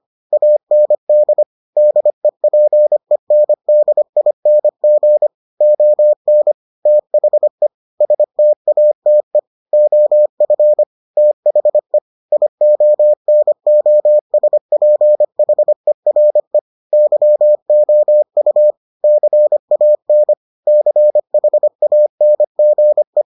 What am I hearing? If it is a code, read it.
Morse code, heard as a keyed tone, about 25 wpm, 600 Hz. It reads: AND DEPENDING ON THE STATE OF THE IONOSPHERE YOU CAN CHANGE